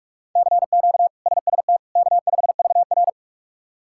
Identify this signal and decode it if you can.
Morse code, heard as a keyed tone, about 46 wpm, 700 Hz. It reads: CQ SST K5VR
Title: K1USN SST QSO Practice - 45wpm